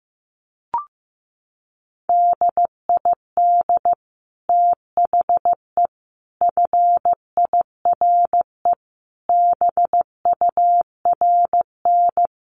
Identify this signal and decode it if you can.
Morse code, heard as a keyed tone, about 15 wpm, 700 Hz. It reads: DID THE FIRE BURN
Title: Sentences from Top 1300 Words 15wpm